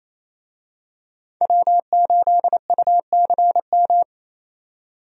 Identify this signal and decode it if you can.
Morse code, heard as a keyed tone, about 28 wpm, 700 Hz. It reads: W8UCM
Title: USA Call Signs - 5 Characters - 28wpm